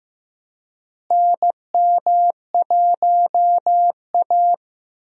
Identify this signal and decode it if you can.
Morse code, heard as a keyed tone, about 15 wpm, 700 Hz. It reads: NM1A
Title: USA Call Signs - 4 Characters - Repeated 2x - 15wpm